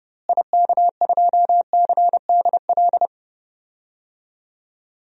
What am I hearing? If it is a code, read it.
Morse code, heard as a keyed tone, about 30 wpm, 700 Hz. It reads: IK2CDL
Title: DXCC-100 Call Signs - 6 Characters - Repeated 3x - 30wpm